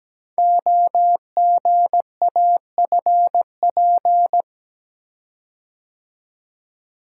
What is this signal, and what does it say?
Morse code, heard as a keyed tone, about 17 wpm, 700 Hz. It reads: OGAFP
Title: Code Groups - 5 Letters - 17wpm